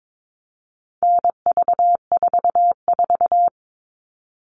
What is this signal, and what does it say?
Morse code, heard as a keyed tone, about 22 wpm, 700 Hz. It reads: NV44
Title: All Cut Numbers - 4-Digit 22wpm